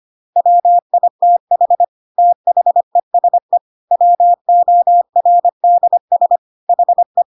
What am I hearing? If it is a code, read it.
Morse code, heard as a keyed tone, about 25 wpm, 700 Hz. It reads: WITH THESE WORDS HE
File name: Pinocchio / Pinocchio ch5 25 25 ik1hge